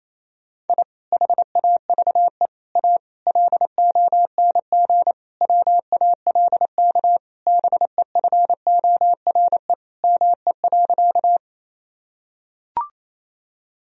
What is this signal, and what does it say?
Morse code, heard as a keyed tone, about 28 wpm, 700 Hz. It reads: I HAVE A LONG WALK BEFORE ME.
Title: Sentences from Top 1000 Words - Review Part 1 - 28wpm